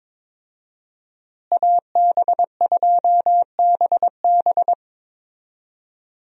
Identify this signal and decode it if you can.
Morse code, heard as a keyed tone, about 22 wpm, 700 Hz. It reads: AB2BB
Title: USA Call Signs - 5 Characters - Repeated 3x - 22wpm